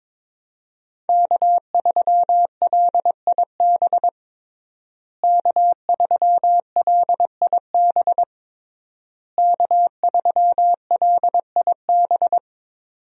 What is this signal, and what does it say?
Morse code, heard as a keyed tone, about 22 wpm, 700 Hz. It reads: K3LIB K3LIB K3LIB
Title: USA Call Signs - 5 Characters - Repeated 3x - 22wpm